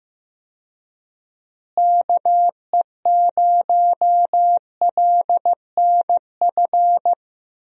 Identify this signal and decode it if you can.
Morse code, heard as a keyed tone, about 15 wpm, 700 Hz. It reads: KE0LNF
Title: USA Call Signs - 6 Characters - Repeated 3x - 15wpm